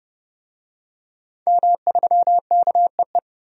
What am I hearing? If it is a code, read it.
Morse code, heard as a keyed tone, about 30 wpm, 700 Hz. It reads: M3KEE
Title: DXCC-100 Call Signs - 5 Characters - 30wpm